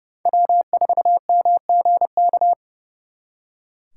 Morse code, heard as a keyed tone, about 37 wpm, 700 Hz. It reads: W4MGK
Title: Speed Racing - USA 1x3 Call Signs 30wpm